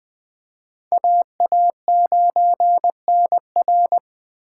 Morse code, heard as a keyed tone, about 20 wpm, 700 Hz. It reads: AA9NR
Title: USA Call Signs - 4x Character Spacing - 20wpm